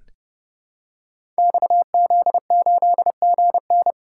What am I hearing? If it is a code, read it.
Morse code, heard as a keyed tone, about 28 wpm, 700 Hz. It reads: XZ8GN
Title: Code Groups - 5 Letter:Numbers - 30wpm